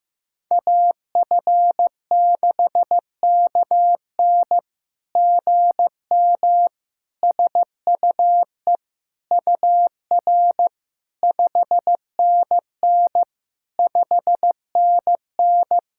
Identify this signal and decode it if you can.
Morse code, heard as a keyed tone, about 15 wpm, 700 Hz. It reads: AF6KN GM SUE UR 5NN 5NN
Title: SOTA QSO Practice 15wpm